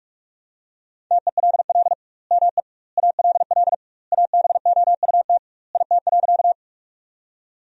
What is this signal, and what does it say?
Morse code, heard as a keyed tone, about 42 wpm, 700 Hz. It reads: TELL ME ALL ABOUT IT.
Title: 2x Word-Spacing - Sentences from Top 200 Words - 45wpm